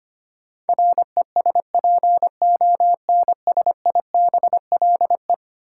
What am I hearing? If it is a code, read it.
Morse code, heard as a keyed tone, about 25 wpm, 700 Hz. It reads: RESPONSIBLE
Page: morse-code-ninja-practice-25wpm